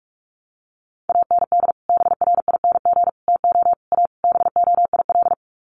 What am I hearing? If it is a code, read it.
Morse code, heard as a keyed tone, about 44 wpm, 700 Hz. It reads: AND BRING TO A BOIL
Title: Sets of 5 Words Encore 45wpm